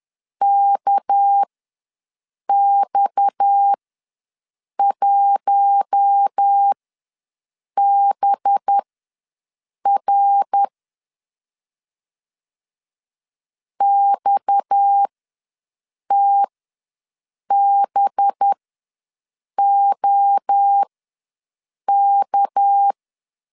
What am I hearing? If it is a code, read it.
Morse code, heard as a keyed tone, about 10 wpm, spaced as for 6 wpm, 800 Hz. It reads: KX1BR XTBOK
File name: LEZIONI / 6 PKX0 / 6 PKX ws 10wpm 3706